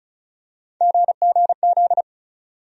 Morse code, heard as a keyed tone, about 36 wpm, 700 Hz. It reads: GGZ
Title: USA Call Sign Postfixes - 3 Letters 35wpm